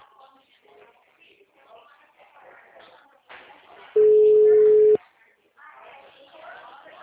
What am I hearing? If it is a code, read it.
Ringback tone (Europe, 425 Hz), with speech.